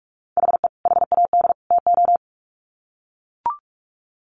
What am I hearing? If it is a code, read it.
Morse code, heard as a keyed tone, about 44 wpm, 700 Hz. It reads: HE HAD TO
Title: Sets of 3 Words 45wpm